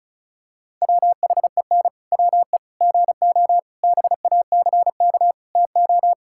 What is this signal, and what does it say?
Morse code, heard as a keyed tone, about 36 wpm, 700 Hz. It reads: WHEN WE GO BACK TO
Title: Pangram Challenge v2 - 35wpm